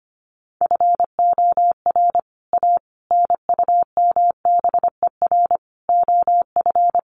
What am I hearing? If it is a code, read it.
Morse code, heard as a keyed tone, about 25 wpm, 700 Hz. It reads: FOR A NUMBER OF
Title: Sets of 5 Words Encore2 25wpm